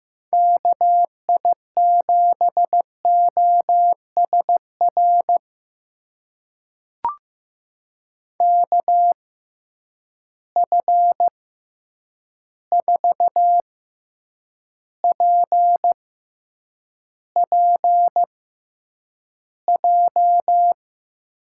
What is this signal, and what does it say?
Morse code, heard as a keyed tone, about 15 wpm, spaced as for 8 wpm, 700 Hz. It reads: KI7OSR KF4PPJ
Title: USA Call Signs - 6 Characters - 6x Character Spacing - 15wpm